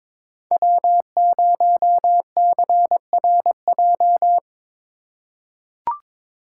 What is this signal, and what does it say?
Morse code, heard as a keyed tone, about 22 wpm, 700 Hz. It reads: W0CRJ
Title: USA Call Signs - 5 Characters - 4x Character Spacing - 22wpm